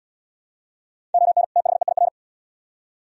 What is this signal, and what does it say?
Morse code, heard as a keyed tone, about 76 wpm, 700 Hz. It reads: ON THEIR